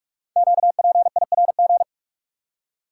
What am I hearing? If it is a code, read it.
Morse code, heard as a keyed tone, about 44 wpm, 700 Hz. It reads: QPIRG
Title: Code Groups - 5 Letters - 45wpm